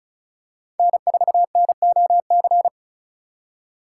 Morse code, heard as a keyed tone, about 34 wpm, 700 Hz. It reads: N4NOC
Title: USA Call Signs - 5 Characters - 4x Character Spacing - 35wpm